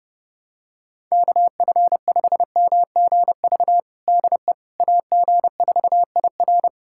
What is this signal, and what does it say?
Morse code, heard as a keyed tone, about 30 wpm, 700 Hz. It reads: KF5MGV DE AG4IR
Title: USA Callsign DE Callsign 30wpm